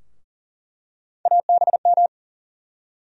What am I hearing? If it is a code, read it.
Morse code, heard as a keyed tone, about 40 wpm, 700 Hz. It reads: ABM
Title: USA Call Sign Postfixes - 3 Letters 40wpm